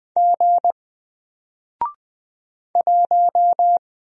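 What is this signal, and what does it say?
Morse code, heard as a keyed tone, about 20 wpm, 700 Hz. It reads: G 1
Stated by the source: Single letter and number Morse drill at 20 wpm, from N4LCD.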